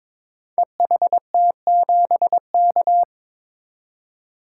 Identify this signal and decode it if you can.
Morse code, heard as a keyed tone, about 23 wpm, 700 Hz. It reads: EHT7K
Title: Code Groups - 5 Letter:Numbers - 22wpm